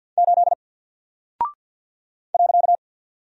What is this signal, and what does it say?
Morse code, heard as a keyed tone, about 50 wpm, 700 Hz. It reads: 8 .
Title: Single Character (Letter:Number:Punctuation:Prosign) 50wpm